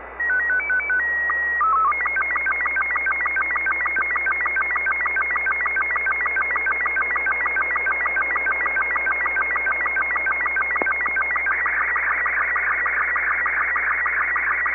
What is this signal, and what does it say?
SSTV (slow-scan television), Robot 36.
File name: sat / ISS 31Jul2011